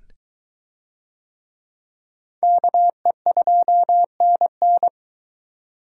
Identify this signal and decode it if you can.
Morse code, heard as a keyed tone, about 20 wpm, 700 Hz. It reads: KE2NN